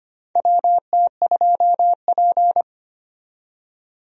Morse code, heard as a keyed tone, about 25 wpm, 700 Hz. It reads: WT2P
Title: USA Call Signs - 4 Characters - 25wpm